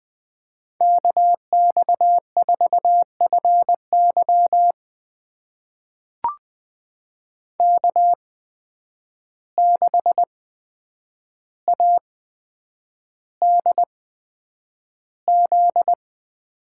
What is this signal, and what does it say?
Morse code, heard as a keyed tone, about 20 wpm, spaced as for 9 wpm, 700 Hz. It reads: KX4FY K6ADZ
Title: USA Call Signs - 8x Character Spacing - 20wpm